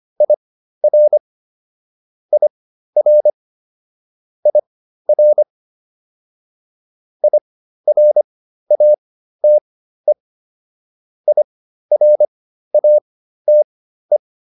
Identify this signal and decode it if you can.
Morse code, heard as a keyed tone, about 25 wpm, spaced as for 13 wpm, 600 Hz. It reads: IR IR IR IRATE IRATE
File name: IR5 13